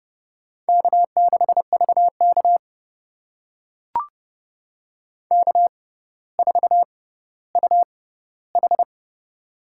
Morse code, heard as a keyed tone, about 30 wpm, spaced as for 15 wpm, 700 Hz. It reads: K6VK K4UH
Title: USA Call Signs - 4 Characters - 6x Character Spacing - 30wpm